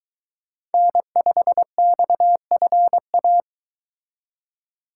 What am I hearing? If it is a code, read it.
Morse code, heard as a keyed tone, about 22 wpm, 700 Hz. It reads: N5XFA